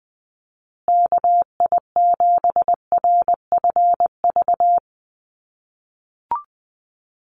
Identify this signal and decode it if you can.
Morse code, heard as a keyed tone, about 20 wpm, 700 Hz. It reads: KI7RFV
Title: USA Call Signs Repeated 2x 20wpm